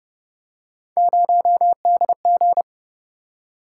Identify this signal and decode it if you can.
Morse code, heard as a keyed tone, about 33 wpm, 700 Hz. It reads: 0DG